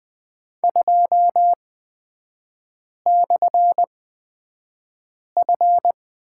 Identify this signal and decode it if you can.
Morse code, heard as a keyed tone, about 20 wpm, spaced as for 5 wpm, 690 Hz. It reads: 2/F
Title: 229 Lesson 37e - Letter x - Three Characters 20wpm